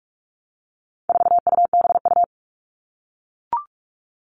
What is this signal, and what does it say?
Morse code, heard as a keyed tone, about 44 wpm, 700 Hz. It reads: 4UBU
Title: All Cut Numbers - 4-Digit 45wpm